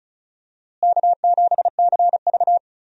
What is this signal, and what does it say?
Morse code, heard as a keyed tone, about 34 wpm, 700 Hz. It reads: K7CV
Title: USA Call Signs - 4 Characters - Repeated 3x - 35wpm